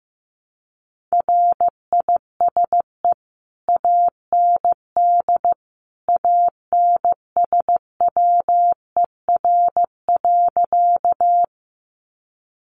Morse code, heard as a keyed tone, about 15 wpm, 700 Hz. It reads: RISE AND ANSWER.
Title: Sentences from Top 900 Words 15wpm